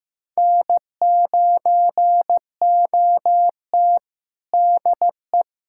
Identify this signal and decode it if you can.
Morse code, heard as a keyed tone, about 15 wpm, 700 Hz. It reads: N9OT DE
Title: USA Callsign DE Callsign - 4 Characters - 15wpm